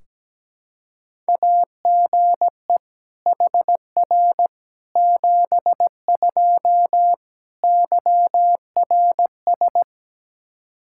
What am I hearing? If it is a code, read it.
Morse code, heard as a keyed tone, about 17 wpm, 700 Hz. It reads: AGE HR 72 YRS
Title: K1USN SST QSO Practice 17wpm